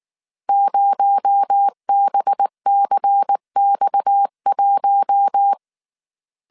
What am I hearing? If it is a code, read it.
Morse code, heard as a keyed tone, about 19 wpm, 800 Hz. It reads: 0BCX1